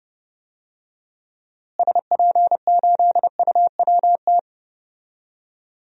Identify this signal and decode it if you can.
Morse code, heard as a keyed tone, about 32 wpm, 700 Hz. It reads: SP8UWT
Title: DXCC-100 Call Signs - 6 Characters - Repeated 2x - 30wpm